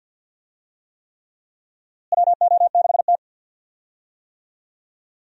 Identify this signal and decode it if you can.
Morse code, heard as a keyed tone, about 48 wpm, 700 Hz. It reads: WO6T
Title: USA Call Signs - 4 Characters - Repeated 3x - 50wpm